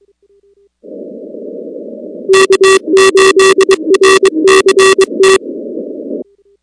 Morse code, heard as a keyed tone, about 22 wpm, 380 Hz. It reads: K8RCT
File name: contests / 2019-cw-wpx-kt5j-k5pi- -2330 0-19